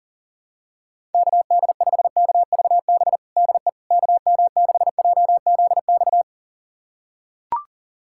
Morse code, heard as a keyed tone, about 40 wpm, 700 Hz. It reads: KD5KVB DE KM6JZX
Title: USA Callsign DE Callsign - 6 Characters - 40wpm